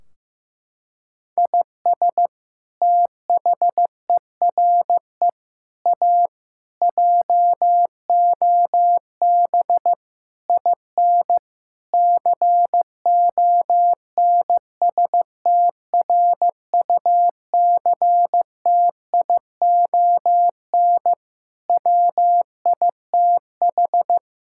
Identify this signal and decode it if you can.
Morse code, heard as a keyed tone, about 15 wpm, 700 Hz. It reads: IS THERE A JOB IN CONSTRUCTION WITH
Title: Sentences from Top 1600 Words 15wpm